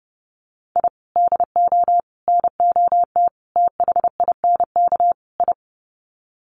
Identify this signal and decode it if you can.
Morse code, heard as a keyed tone, about 29 wpm, 700 Hz. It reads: I DO NOT THINK I
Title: Sets of 5 Words 30wpm